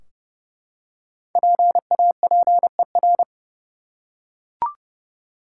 Morse code, heard as a keyed tone, about 31 wpm, 700 Hz. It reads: PAPER
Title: Five Letter Words - 30wpm